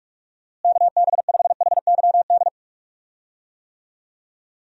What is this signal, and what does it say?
Morse code, heard as a keyed tone, about 45 wpm, 700 Hz. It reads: KB5HYD
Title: USA Call Signs - 6 Characters - Repeated 2x - 45wpm